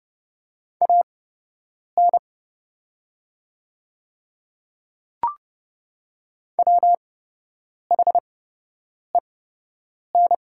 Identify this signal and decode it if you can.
Morse code, heard as a keyed tone, about 30 wpm, spaced as for 8 wpm, 700 Hz. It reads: AN WHEN
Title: 8x Character Spacing - Top 300 Words - 30wpm